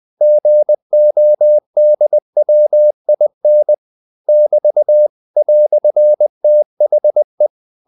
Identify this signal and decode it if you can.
Morse code, heard as a keyed tone, about 20 wpm, 600 Hz. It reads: GODWIN <BT> "THE
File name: CW / Practice / EnglishQuotations0320